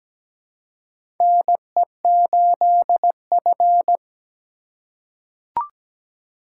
Morse code, heard as a keyed tone, about 17 wpm, 700 Hz. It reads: NE8F
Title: USA Call Signs - 4 Characters - 3x Character Spacing - 17wpm